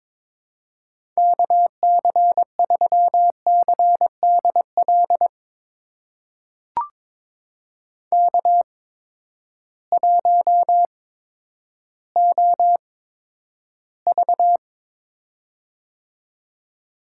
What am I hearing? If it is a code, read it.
Morse code, heard as a keyed tone, about 22 wpm, spaced as for 11 wpm, 700 Hz. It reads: KC3CDL K1OV
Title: USA Call Signs - 8x Character Spacing - 22wpm